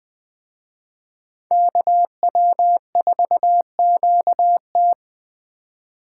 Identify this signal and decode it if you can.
Morse code, heard as a keyed tone, about 20 wpm, 700 Hz. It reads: KW4QT